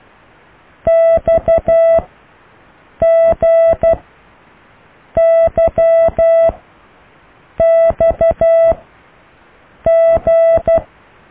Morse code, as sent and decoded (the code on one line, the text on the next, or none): -..- --. -.-- -..- --.
XGYXG